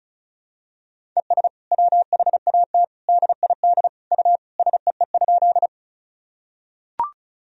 Morse code, heard as a keyed tone, about 34 wpm, 700 Hz. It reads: ES WHAT DID U SEE?
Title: Sentences from Top 100 Words and Abbreviations 35wpm